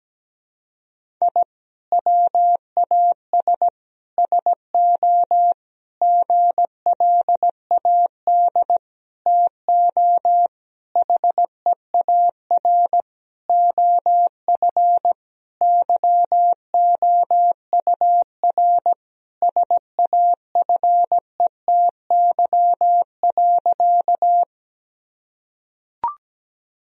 Morse code, heard as a keyed tone, about 17 wpm, 700 Hz. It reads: I WAS SO GLAD TO HEAR OF YOUR SAFETY.
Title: Sentences from Top 1000 Words - Review Part 4 - 17wpm